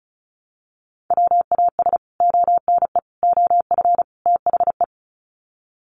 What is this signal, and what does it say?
Morse code, heard as a keyed tone, about 34 wpm, 700 Hz. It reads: WAS ONE OF THE
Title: Sets of 4 Words 35wpm